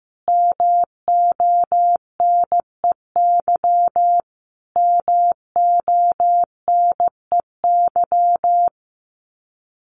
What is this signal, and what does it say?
Morse code, heard as a keyed tone, about 15 wpm, 700 Hz. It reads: MONEY MONEY
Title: Top 500 Words Repeated 15wpm